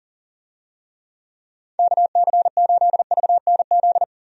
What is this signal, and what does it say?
Morse code, heard as a keyed tone, about 40 wpm, 700 Hz. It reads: KC8VNZ
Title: USA Call Signs - 6 Characters - 40wpm